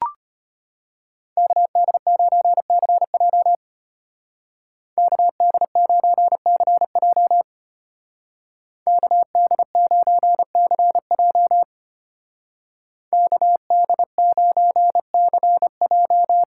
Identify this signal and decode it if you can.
Morse code, heard as a keyed tone, about 31 wpm, 700 Hz. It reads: KD9CJ KD9CJ KD9CJ KD9CJ
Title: Speed Racing - USA Call Signs 25wpm